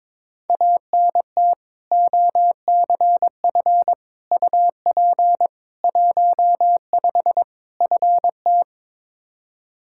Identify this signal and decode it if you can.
Morse code, heard as a keyed tone, about 22 wpm, 700 Hz. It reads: ANT OCF UP 15 FT
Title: Practice QSO - Exchange 2 - Info - 22wpm